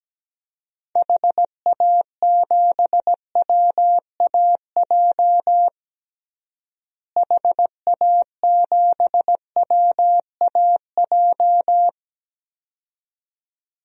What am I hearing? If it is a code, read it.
Morse code, heard as a keyed tone, about 17 wpm, 700 Hz. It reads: HA7WAJ HA7WAJ
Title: DXCC-100 Call Signs - 6 Characters - Repeated 2x - 17wpm